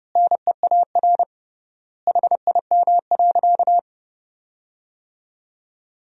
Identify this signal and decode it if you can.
Morse code, heard as a keyed tone, about 30 wpm, 700 Hz. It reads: NEAR HIM.
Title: 3x Word-Spacing - Sentences from Top 200 Words - 30wpm